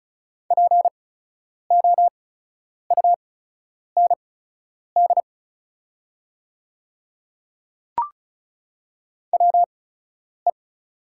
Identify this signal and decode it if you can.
Morse code, heard as a keyed tone, about 36 wpm, spaced as for 10 wpm, 700 Hz. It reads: POUND WE
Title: 8x Character Spacing - Top 500 Words - 35wpm